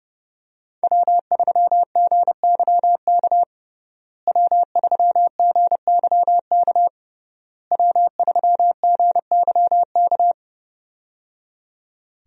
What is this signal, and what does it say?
Morse code, heard as a keyed tone, about 30 wpm, 700 Hz. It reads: W3GYK W3GYK W3GYK
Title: USA Call Signs - 5 Characters - Repeated 3x - 30wpm